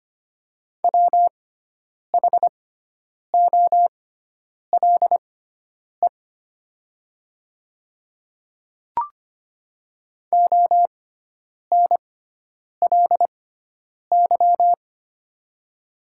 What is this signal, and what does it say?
Morse code, heard as a keyed tone, about 25 wpm, spaced as for 9 wpm, 700 Hz. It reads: WHOLE ONLY
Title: 6x Character Spacing - Top 500 Words - 25wpm